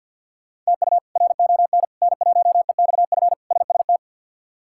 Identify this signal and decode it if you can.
Morse code, heard as a keyed tone, about 48 wpm, 700 Hz. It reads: TU RON N1EXF SST
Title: K1USN SST QSO Practice - 50wpm